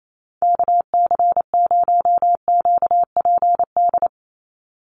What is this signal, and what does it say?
Morse code, heard as a keyed tone, about 28 wpm, 700 Hz. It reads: KC0QPD
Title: USA Call Signs 28wpm